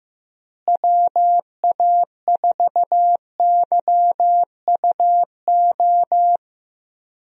Morse code, heard as a keyed tone, about 15 wpm, 700 Hz. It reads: WA4YUO
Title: USA Call Signs - 6 Characters - Repeated 2x - 15wpm